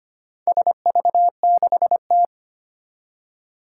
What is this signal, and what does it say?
Morse code, heard as a keyed tone, about 25 wpm, 700 Hz. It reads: SV6T
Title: Four Challenging Letter-Number Combinations 25wpm